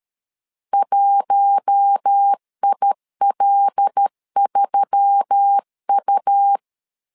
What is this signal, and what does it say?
Morse code, heard as a keyed tone, about 13 wpm, 800 Hz. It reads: 1IL3U
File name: LEZIONI / 5 LBVC / 5 L 12wpm 1913